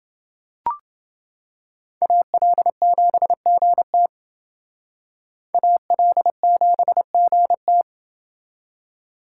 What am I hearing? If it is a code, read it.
Morse code, heard as a keyed tone, about 28 wpm, 700 Hz. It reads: AL7GT AL7GT